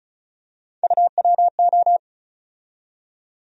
Morse code, heard as a keyed tone, about 36 wpm, 700 Hz. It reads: UWO
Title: USA Call Sign Postfixes - 3 Letters 35wpm